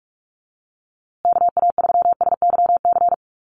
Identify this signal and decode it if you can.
Morse code, heard as a keyed tone, about 44 wpm, 700 Hz. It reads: KA3SYC
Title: USA Call Signs Repeated 2x 45wpm